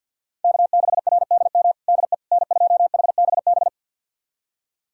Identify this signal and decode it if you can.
Morse code, heard as a keyed tone, about 48 wpm, 700 Hz. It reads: K6RDM DE N2HBB
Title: USA Callsign DE Callsign - 5 Characters - 50wpm